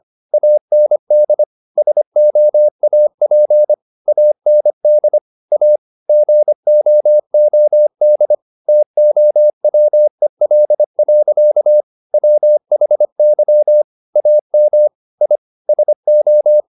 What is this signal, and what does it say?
Morse code, heard as a keyed tone, about 25 wpm, 600 Hz. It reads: AND SOAP AND A GOOD TOWEL. WHY AM I SO
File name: the old man and the sea part 1 25wpm